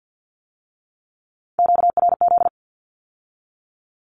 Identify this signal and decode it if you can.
Morse code, heard as a keyed tone, about 50 wpm, 700 Hz. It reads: QRZ
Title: Q-Codes with Spoken Meaning - 50wpm